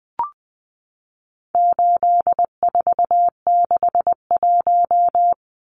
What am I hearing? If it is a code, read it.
Morse code, heard as a keyed tone, about 20 wpm, 700 Hz. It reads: 8461
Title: Numbers - 4-Digit 20wpm